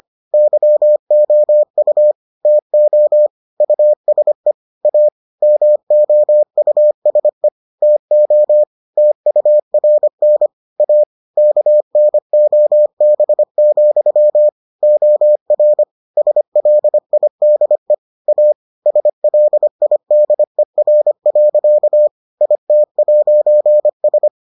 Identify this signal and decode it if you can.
Morse code, heard as a keyed tone, about 25 wpm, 600 Hz. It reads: YOU TO USE A MOUSE TO TURN A KNOB, OR SLIDE A SLIDER. IT'S